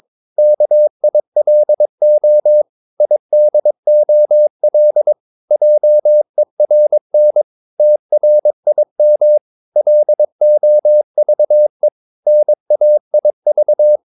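Morse code, heard as a keyed tone, about 22 wpm, 600 Hz. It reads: KILO IDOL JERN TRIM LOVE NAIV
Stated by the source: Morse practice of short Danish words at 22 wpm, from CW med Gnister.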